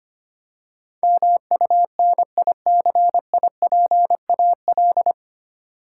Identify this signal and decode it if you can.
Morse code, heard as a keyed tone, about 25 wpm, 700 Hz. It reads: MUNICIPAL
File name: practice / 25wpm / 9 Letter Words - First 3 Letters Given - 25wpm